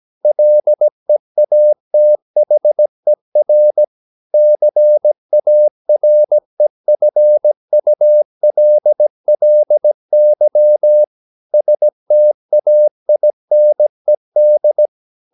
Morse code, heard as a keyed tone, about 17 wpm, 600 Hz. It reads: LEATHER CAREFULLY STAINED
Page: Romanen Never af Ken Follet på engelsk.